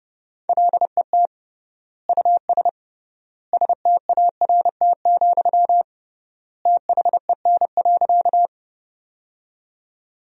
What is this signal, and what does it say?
Morse code, heard as a keyed tone, about 32 wpm, 700 Hz. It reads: LET US START, THEN.
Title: 3x Word-Spacing - Sentences from Top 300 Words - 30wpm